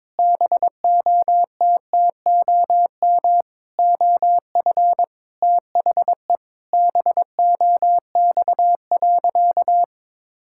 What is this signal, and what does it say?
Morse code, heard as a keyed tone, about 22 wpm, 700 Hz. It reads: BOTTOM OF THE BOX.
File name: practice / 22wpm / Sentences with 4-Word Phrases - 22wpm